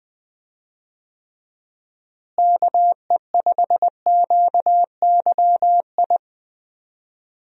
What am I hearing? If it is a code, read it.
Morse code, heard as a keyed tone, about 20 wpm, 700 Hz. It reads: KE5QYI
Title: USA Call Signs - 6 Characters - Repeated 2x - 20wpm